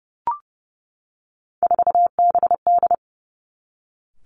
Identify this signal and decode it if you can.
Morse code, heard as a keyed tone, about 30 wpm, 700 Hz. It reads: 4BD
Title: Three Challenging Letter-Number Combinations 30wpm